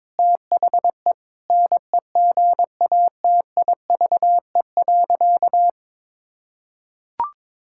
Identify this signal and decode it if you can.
Morse code, heard as a keyed tone, about 22 wpm, 700 Hz. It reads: THE NEGATIVE.
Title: Sentences from Top 1000 Words - Review Part 6 - 22wpm